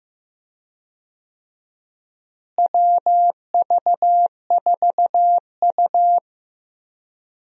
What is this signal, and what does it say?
Morse code, heard as a keyed tone, about 15 wpm, 700 Hz. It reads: WV4U
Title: USA Call Signs - 4 Characters - Repeated 2x - 15wpm